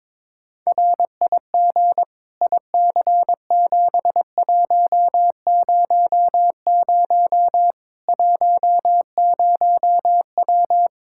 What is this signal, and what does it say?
Morse code, heard as a keyed tone, about 22 wpm, 700 Hz. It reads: RIG IC7100 10W
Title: Practice QSO - Exchange 2 - Info - 22wpm